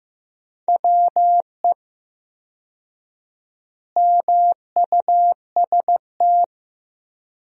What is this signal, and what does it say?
Morse code, heard as a keyed tone, about 16 wpm, 700 Hz. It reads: WE MUST
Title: 4x Word-Spacing - Sentences from Top 200 Words - 15wpm